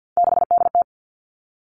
Morse code, heard as a keyed tone, about 50 wpm, 700 Hz. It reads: 6DT
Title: Three Challenging Letter-Number Combinations 50wpm